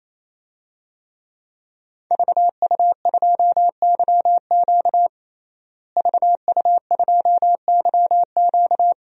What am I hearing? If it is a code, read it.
Morse code, heard as a keyed tone, about 27 wpm, 700 Hz. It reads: VU2YQ VU2YQ
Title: DXCC-100 Call Signs - 5 Characters - Repeated 3x - 28wpm